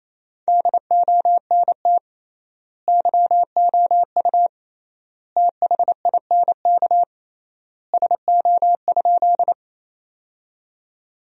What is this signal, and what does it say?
Morse code, heard as a keyed tone, about 27 wpm, 700 Hz. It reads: DONT YOU THINK SO?
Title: 3x Word-Spacing - Sentences from Top 300 Words - 28wpm